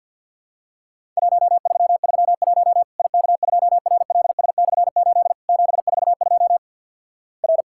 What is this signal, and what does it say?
Morse code, heard as a keyed tone, about 50 wpm, 700 Hz. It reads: 1331 IK2RLS/8 742 R
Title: Worked All Europe DX Contest - QTC Exchange - 50wpm